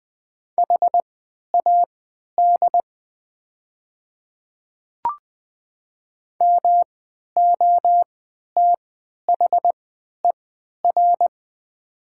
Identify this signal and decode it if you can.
Morse code, heard as a keyed tone, about 20 wpm, spaced as for 11 wpm, 700 Hz. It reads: HAD MOTHER
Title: 3x Character Spacing - Top 200 Words - 20wpm